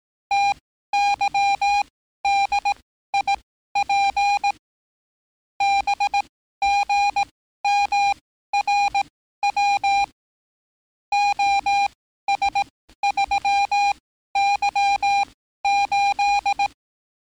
Morse code, heard as a keyed tone, about 18 wpm, spaced as for 13 wpm, 800 Hz. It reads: TYDIP BGMRW OS3Y8
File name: training / NAVPERS 11049to11060 / NAVPERS 11054 RA RB